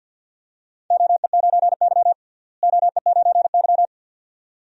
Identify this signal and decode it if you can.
Morse code, heard as a keyed tone, about 52 wpm, 700 Hz. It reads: OE9Y OE9Y
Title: DXCC-100 Call Signs - 4 Characters - Repeated 2x - 50wpm